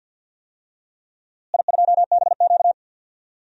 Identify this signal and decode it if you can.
Morse code, heard as a keyed tone, about 53 wpm, 700 Hz. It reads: I2BQ
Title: DXCC-100 Call Signs - 4 Characters - Repeated 3x - 50wpm